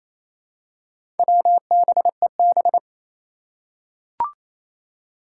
Morse code, heard as a keyed tone, about 28 wpm, 700 Hz. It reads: WBEB
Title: Four Challenging Letter-Number Combinations 28wpm